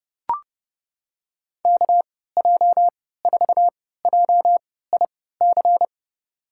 Morse code, heard as a keyed tone, about 31 wpm, spaced as for 17 wpm, 700 Hz. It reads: KJ4JIC
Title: USA Call Signs - 6 Characters - 3x Character Spacing - 30wpm